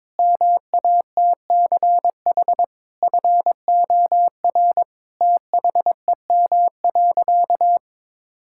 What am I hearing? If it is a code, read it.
Morse code, heard as a keyed tone, about 22 wpm, 700 Hz. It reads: MATCH FOR THEM.
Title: Sentences from Top 1600 Words 22wpm